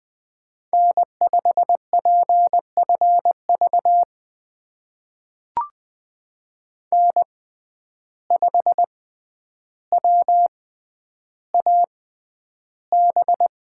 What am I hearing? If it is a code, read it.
Morse code, heard as a keyed tone, about 20 wpm, spaced as for 10 wpm, 700 Hz. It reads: N5PFV N5WAB
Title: USA Call Signs - 5 Characters - 6x Character Spacing - 20wpm